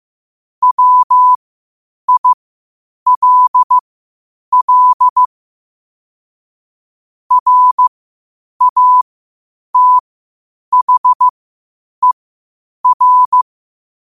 Morse code, heard as a keyed tone, about 15 wpm, spaced as for 8 wpm, 1000 Hz. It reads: WILL RATHER
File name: qotdfiles / 8wpm